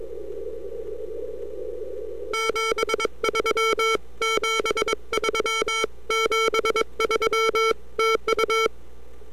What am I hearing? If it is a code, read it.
Morse code, heard as a keyed tone, about 22 wpm, 1340 Hz. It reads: 73 73 73 TU
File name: n6tr / F8BPN-20Oct2009